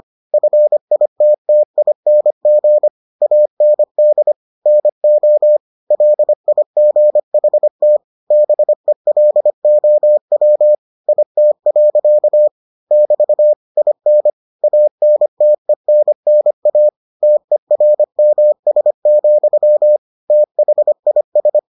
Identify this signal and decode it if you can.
Morse code, heard as a keyed tone, about 25 wpm, 600 Hz. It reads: FITTING AND NO LIGHT BELOW IT. <BT> IN ANTENNA TERMS, THIS